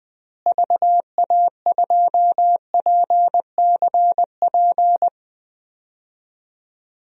Morse code, heard as a keyed tone, about 20 wpm, 700 Hz. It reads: VA2PCP
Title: DXCC-100 Call Signs - 6 Characters - Repeated 3x - 20wpm